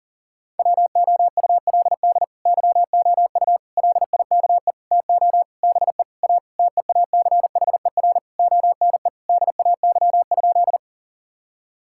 Morse code, heard as a keyed tone, about 37 wpm, 700 Hz. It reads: WOULD YOU LIKE TO BE A TEACHER ONE DAY?
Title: Sentences from Top 1000 Words - Review Part 6 - 40wpm